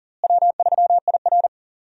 Morse code, heard as a keyed tone, about 40 wpm, 700 Hz. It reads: W3IR
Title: USA Call Signs - 4 Characters - Repeated 3x - 40wpm